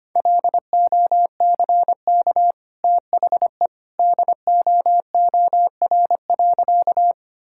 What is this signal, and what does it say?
Morse code, heard as a keyed tone, about 25 wpm, 700 Hz. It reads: LOCK THE DOOR.
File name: Easy Sentences with Hints - 25wpm